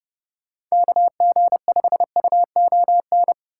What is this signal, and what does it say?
Morse code, heard as a keyed tone, about 40 wpm, spaced as for 35 wpm, 700 Hz. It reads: KG5UON
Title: Speed-Racing Common QSO Phrases - 30wpm